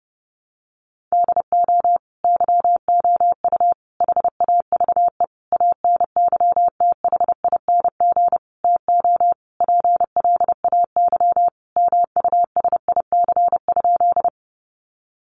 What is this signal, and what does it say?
Morse code, heard as a keyed tone, about 30 wpm, 700 Hz. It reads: DO YOU HAVE ANYTHING TO PLAY MUSIC?
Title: Sentences from Top 1000 Words 30wpm